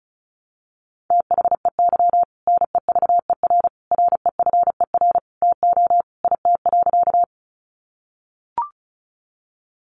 Morse code, heard as a keyed tone, about 34 wpm, 700 Hz. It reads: THEY NEVER REFER TO IT.
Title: Sentences from Top 1000 Words 35wpm